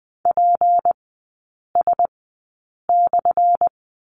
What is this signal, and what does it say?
Morse code, heard as a keyed tone, about 20 wpm, spaced as for 8 wpm, 700 Hz. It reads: PS/
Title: 178 Lesson 29e - Slash - Three Characters 20wpm